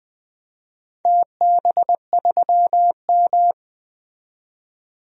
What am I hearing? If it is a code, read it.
Morse code, heard as a keyed tone, about 20 wpm, 700 Hz. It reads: TB3M
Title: DXCC-100 Call Signs - 4 Characters - 20wpm